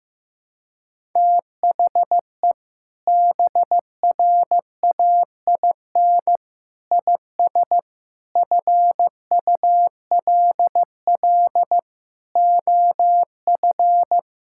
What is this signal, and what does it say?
Morse code, heard as a keyed tone, about 15 wpm, 700 Hz. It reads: THE BRAIN IS FULL OF
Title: Sentences from Top 1000 Words - Review Part 6 - 15wpm